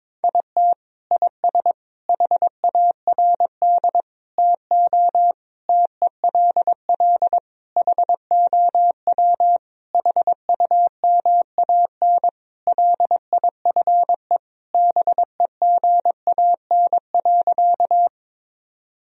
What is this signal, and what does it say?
Morse code, heard as a keyed tone, about 22 wpm, 700 Hz. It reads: IT IS HARD TO TELL HOW HUMAN LIFE BEGAN.
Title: Sentences from Top 1000 Words - Review Part 3 - 22wpm